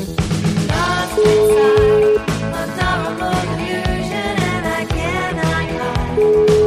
Ringback tone (Europe, 425 Hz), with noise.